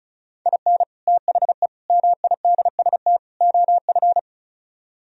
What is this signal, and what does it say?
Morse code, heard as a keyed tone, about 47 wpm, spaced as for 41 wpm, 700 Hz. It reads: IN THE MIDST OF
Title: Speed-Racing - Sets of 4 Words 35wpm